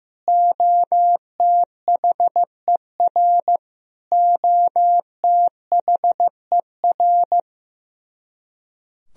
Morse code, heard as a keyed tone, about 15 wpm, 700 Hz. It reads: OTHER OTHER
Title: Top 100 Words Repeated 15wpm